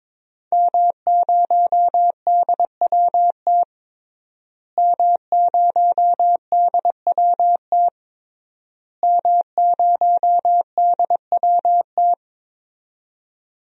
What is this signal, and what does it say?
Morse code, heard as a keyed tone, about 22 wpm, 700 Hz. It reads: M0DWT M0DWT M0DWT
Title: DXCC-100 Call Signs - 5 Characters - Repeated 3x - 22wpm